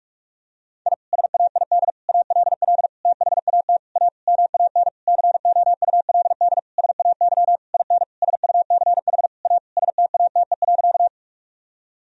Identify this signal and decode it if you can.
Morse code, heard as a keyed tone, about 46 wpm, 700 Hz. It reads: I SAID ALL THAT A MAN COULD SAY IN SUCH A STATE.
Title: Sentences from Top 1000 Words - Review Part 2 - 45wpm